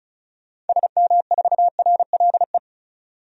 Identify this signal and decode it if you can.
Morse code, heard as a keyed tone, about 36 wpm, 700 Hz. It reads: SM4RLE
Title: DXCC-100 Call Signs - 6 Characters - 35wpm